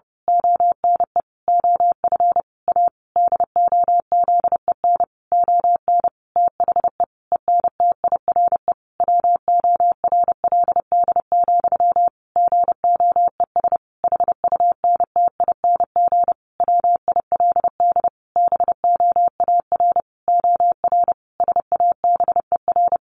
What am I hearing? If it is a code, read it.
Morse code, heard as a keyed tone, about 30 wpm, 700 Hz. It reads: ONE OF A DOZEN ON THE ENTIRE WORLD, GOES HUNTING WILD BOAR OR SABER
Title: A Sound Of Thunder 30wpm